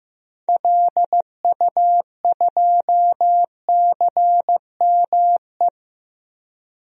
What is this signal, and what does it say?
Morse code, heard as a keyed tone, about 15 wpm, 700 Hz. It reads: LU2CME
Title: DXCC-100 Call Signs - 6 Characters - Repeated 3x - 15wpm